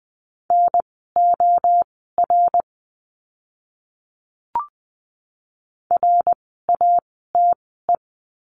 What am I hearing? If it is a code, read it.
Morse code, heard as a keyed tone, about 20 wpm, spaced as for 14 wpm, 700 Hz. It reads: NOR RATE
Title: 054 Lesson 10f - Letter r - Words 20wpm